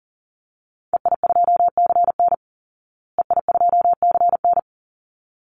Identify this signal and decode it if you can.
Morse code, heard as a keyed tone, about 40 wpm, 700 Hz. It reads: EI2CN EI2CN
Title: Call Signs Repeated 3x - 40wpm